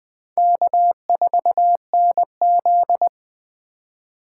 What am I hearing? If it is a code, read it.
Morse code, heard as a keyed tone, about 20 wpm, 700 Hz. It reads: K4NZ
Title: Speed Racing - USA 1x2 Call Signs 17wpm